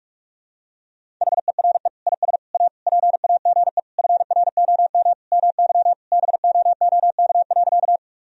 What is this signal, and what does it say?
Morse code, heard as a keyed tone, about 43 wpm, 700 Hz. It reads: HERE IS A PAGE FROM MY BOOK.
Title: Sentences from Top 1000 Words - Review Part 1 - 45wpm